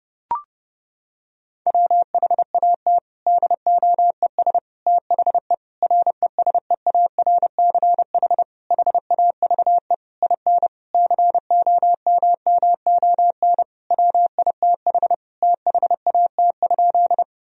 Morse code, heard as a keyed tone, about 30 wpm, 700 Hz. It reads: WHAT DOES THE RESEARCH HAVE IN COMMON WITH THAT?
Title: Sentences from Top 1000 Words - Review Part 4 - 30wpm